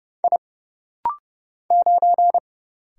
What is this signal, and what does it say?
Morse code, heard as a keyed tone, about 30 wpm, 700 Hz. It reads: I 9
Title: Single Letter-Number - Rapid-Fire 30wpm